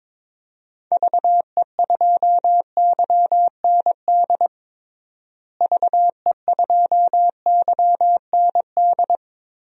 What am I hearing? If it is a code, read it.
Morse code, heard as a keyed tone, about 22 wpm, 700 Hz. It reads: VE2YND VE2YND
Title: DXCC-100 Call Signs - 6 Characters - Repeated 3x - 22wpm